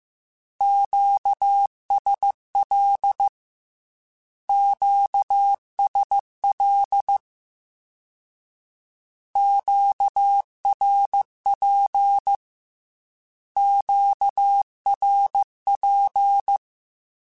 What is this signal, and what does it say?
Morse code, heard as a keyed tone, about 15 wpm, 800 Hz. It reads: QSL QSL QRP QRP